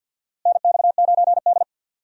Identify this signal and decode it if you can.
Morse code, heard as a keyed tone, about 52 wpm, 700 Hz. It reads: NX9D
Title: USA Call Signs - 4 Characters - Repeated 2x - 50wpm